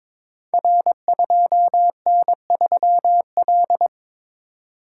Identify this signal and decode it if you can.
Morse code, heard as a keyed tone, about 22 wpm, 700 Hz. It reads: R2N3L
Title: Code Groups - 5 Letter:Numbers - 22wpm